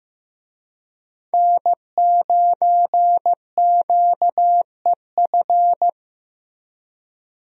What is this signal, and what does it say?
Morse code, heard as a keyed tone, about 15 wpm, 700 Hz. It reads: N9QEF
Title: USA Call Signs - 5 Characters - Repeated 2x - 15wpm